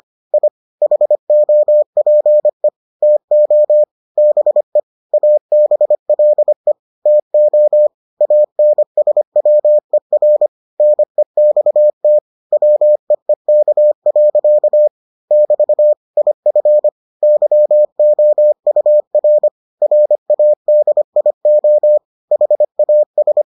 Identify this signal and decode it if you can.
Morse code, heard as a keyed tone, about 25 wpm, 600 Hz. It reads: I HOPE TO BE ABLE TO ANSWER NEXT WEEK. <BT> IF YOUR RADIO HAS